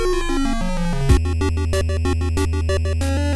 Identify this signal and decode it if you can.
There is no code.